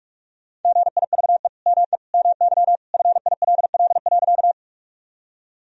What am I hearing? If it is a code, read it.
Morse code, heard as a keyed tone, about 46 wpm, 700 Hz. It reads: GIVE ME MY FILL.
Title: Sentences from Top 1000 Words - Review Part 3 - 45wpm